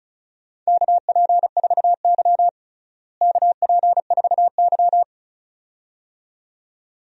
Morse code, heard as a keyed tone, about 36 wpm, 700 Hz. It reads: KP4Y KP4Y
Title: DXCC-100 Call Signs - 4 Characters - Repeated 2x - 35wpm